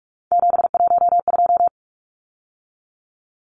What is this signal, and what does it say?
Morse code, heard as a keyed tone, about 44 wpm, 700 Hz. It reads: 712
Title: Numbers - 3-Digit 45wpm